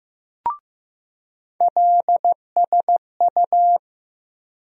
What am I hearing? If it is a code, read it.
Morse code, heard as a keyed tone, about 15 wpm, 700 Hz. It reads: LSU
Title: USA Call Sign Postfixes - 3 Letters 15wpm